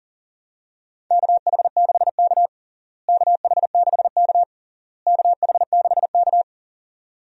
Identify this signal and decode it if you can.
Morse code, heard as a keyed tone, about 40 wpm, 700 Hz. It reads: KH6K KH6K KH6K
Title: DXCC-100 Call Signs - 4 Characters - Repeated 3x - 40wpm